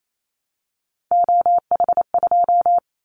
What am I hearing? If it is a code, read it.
Morse code, heard as a keyed tone, about 28 wpm, 700 Hz. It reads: OH2
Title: Callsign Prefixes for Top 25 DXCC - Repeated 3x - 28wpm